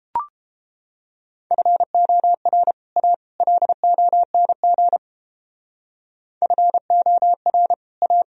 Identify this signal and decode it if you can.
Morse code, heard as a keyed tone, about 31 wpm, 700 Hz. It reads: FOR A LONG FOR A